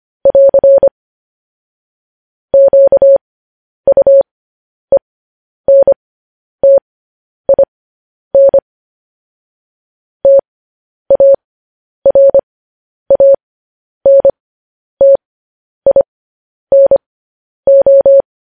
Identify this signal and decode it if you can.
Morse code, heard as a keyed tone, about 25 wpm, spaced as for 10 wpm, 550 Hz. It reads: <AR> QUENTIN TARANTINO
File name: podcast / News headlines-1x-25x10